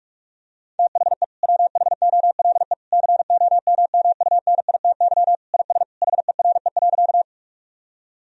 Morse code, heard as a keyed tone, about 46 wpm, 700 Hz. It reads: THE WHOLE COMMUNITY IS HERE.
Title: Sentences from Top 1000 Words - Review Part 4 - 45wpm